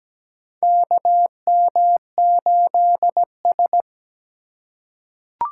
Morse code, heard as a keyed tone, about 17 wpm, 700 Hz. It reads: KM8S
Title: USA Call Signs - 4 Characters - 4x Character Spacing - 17wpm